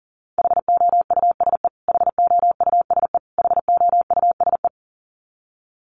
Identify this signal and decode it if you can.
Morse code, heard as a keyed tone, about 40 wpm, 700 Hz. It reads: HOUSE HOUSE HOUSE
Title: Top 500 Words Repeated 40wpm